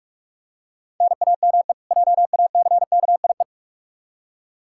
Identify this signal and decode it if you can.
Morse code, heard as a keyed tone, about 46 wpm, 700 Hz. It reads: NAME JACKIE
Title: Practice QSO - Exchange 1 - Beginning - 45wpm